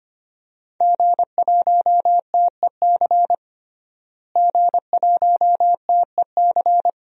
Morse code, heard as a keyed tone, about 25 wpm, 700 Hz. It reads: G1TEC G1TEC
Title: DXCC-100 Call Signs - 5 Characters - Repeated 3x - 25wpm